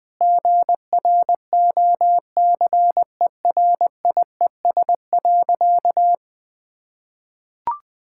Morse code, heard as a keyed tone, about 21 wpm, 700 Hz. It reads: GROCERIES.
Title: Sentences with 3-Word Phrases - 20wpm